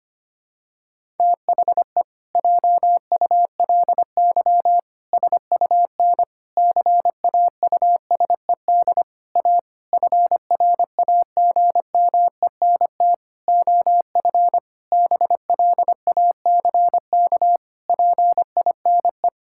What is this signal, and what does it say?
Morse code, heard as a keyed tone, about 25 wpm, 700 Hz. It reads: THE JULY SUN CAUSED A FRAGMENT OF BLACK PINE
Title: Pangram Challenge v2 - 25wpm